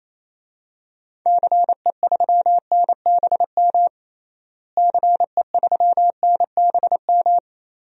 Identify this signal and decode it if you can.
Morse code, heard as a keyed tone, about 28 wpm, 700 Hz. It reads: CE3NBM CE3NBM
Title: DXCC-100 Call Signs - 6 Characters - Repeated 3x - 28wpm